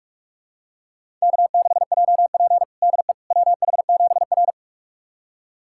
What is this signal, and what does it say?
Morse code, heard as a keyed tone, about 42 wpm, 700 Hz. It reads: K6JP DE WH7R
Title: USA Callsign DE Callsign - 4 Characters - 45wpm